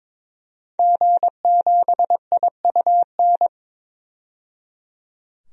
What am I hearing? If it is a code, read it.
Morse code, heard as a keyed tone, about 21 wpm, 700 Hz. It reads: G7IUN
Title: DXCC-100 Call Signs - 5 Characters - 22wpm